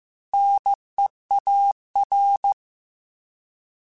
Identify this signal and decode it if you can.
Morse code, heard as a keyed tone, about 15 wpm, 800 Hz. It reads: NEAR
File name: Alphabetical / Alphabetical mp3s / Ns @13